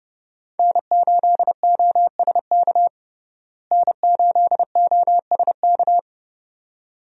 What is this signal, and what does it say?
Morse code, heard as a keyed tone, about 32 wpm, 700 Hz. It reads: N8OSK N8OSK
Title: USA Call Signs - 5 Characters - Repeated 2x - 30wpm